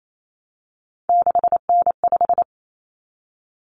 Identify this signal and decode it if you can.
Morse code, heard as a keyed tone, about 28 wpm, 700 Hz. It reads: 6N5
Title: Common Cut Numbers - 3-Digit 28wpm